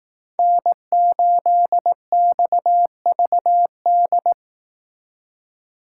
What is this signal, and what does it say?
Morse code, heard as a keyed tone, about 19 wpm, 700 Hz. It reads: N8XVD